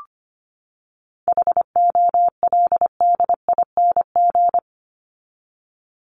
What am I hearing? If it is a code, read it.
Morse code, heard as a keyed tone, about 25 wpm, 700 Hz. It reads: HOLDING
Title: ING Suffix - Top 100 Words with Base Words 25wpm